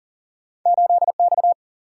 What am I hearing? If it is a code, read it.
Morse code, heard as a keyed tone, about 42 wpm, 700 Hz. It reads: 8X
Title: Single-Letter Single-Number 40wpm